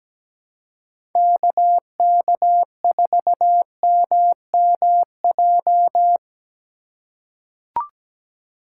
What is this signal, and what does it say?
Morse code, heard as a keyed tone, about 17 wpm, 700 Hz. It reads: KK4MMJ
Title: USA Call Signs - 3x Character Spacing - 17wpm